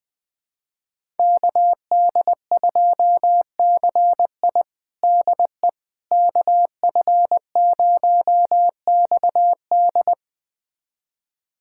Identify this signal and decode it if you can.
Morse code, heard as a keyed tone, about 20 wpm, 700 Hz. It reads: KD2CI DE KF0XD
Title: USA Callsign DE Callsign - 5 Characters - 20wpm